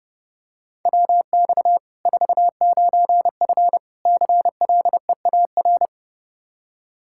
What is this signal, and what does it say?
Morse code, heard as a keyed tone, about 30 wpm, 700 Hz. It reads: WX 49F CLEAR
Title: Practice QSO - Exchange 2 - Info - 30wpm